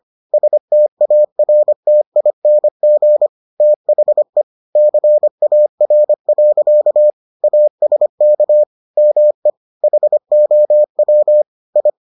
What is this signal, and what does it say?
Morse code, heard as a keyed tone, about 25 wpm, 600 Hz. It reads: STARTING THE CAR. ASK ME HOW I